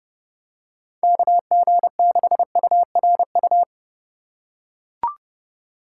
Morse code, heard as a keyed tone, about 32 wpm, 700 Hz. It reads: KG6URU
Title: USA Call Signs - 4x Character Spacing - 30wpm